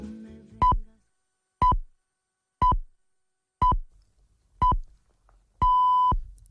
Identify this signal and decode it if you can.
Radio time signal (time pips). It is heard from 0.6 to 6.1 s.